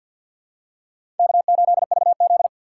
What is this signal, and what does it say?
Morse code, heard as a keyed tone, about 49 wpm, 700 Hz. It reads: K8VZ
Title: USA Call Signs - 4 Characters - 3x Character Spacing - 50wpm